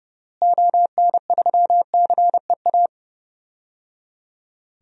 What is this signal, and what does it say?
Morse code, heard as a keyed tone, about 30 wpm, 700 Hz. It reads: ON3CEA
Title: DXCC-100 Call Signs - 6 Characters - Repeated 3x - 30wpm